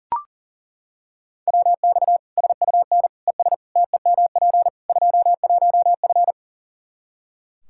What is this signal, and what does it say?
Morse code, heard as a keyed tone, about 40 wpm, 700 Hz. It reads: WX SUN ES TEMP 21F
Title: Practice QSO - Exchange 2 - Info - 40wpm